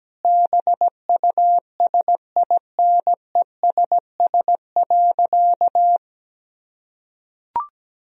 Morse code, heard as a keyed tone, about 17 wpm, 700 Hz. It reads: BUSINESS.
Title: Sentences from Top 1000 Words - Review Part 4 - 17wpm